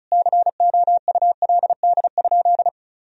Morse code, heard as a keyed tone, about 36 wpm, 700 Hz. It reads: COULD?
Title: Sentences from Top 400 Words and Abbreviations 35wpm